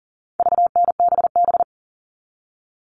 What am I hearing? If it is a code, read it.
Morse code, heard as a keyed tone, about 40 wpm, 700 Hz. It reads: VNBB
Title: All Cut Numbers - 4-Digit 40wpm